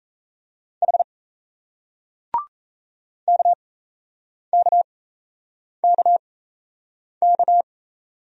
Morse code, heard as a keyed tone, about 38 wpm, spaced as for 8 wpm, 700 Hz. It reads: H KKKK